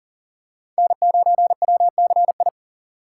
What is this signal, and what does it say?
Morse code, heard as a keyed tone, about 40 wpm, 700 Hz. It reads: N9WCI
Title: USA Call Signs - 5 Characters - 6x Character Spacing - 40wpm